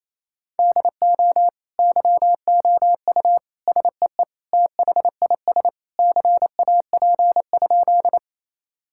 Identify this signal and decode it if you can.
Morse code, heard as a keyed tone, about 27 wpm, 700 Hz. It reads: DO YOU SEE THIS CAP?
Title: Sentences from Top 1600 Words 28wpm